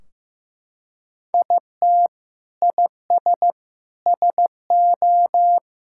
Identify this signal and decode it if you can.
Morse code, heard as a keyed tone, about 15 wpm, 700 Hz. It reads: IT IS SO
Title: Sentences from Top 1600 Words 15wpm